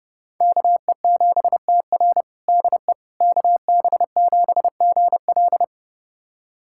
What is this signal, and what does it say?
Morse code, heard as a keyed tone, about 30 wpm, 700 Hz. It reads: KE7TR DE KB7GL
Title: USA Callsign DE Callsign - 5 Characters - 30wpm